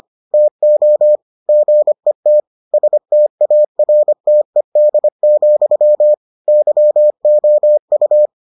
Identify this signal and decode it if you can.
Morse code, heard as a keyed tone, about 25 wpm, 600 Hz. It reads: TO GET STARTED, YOU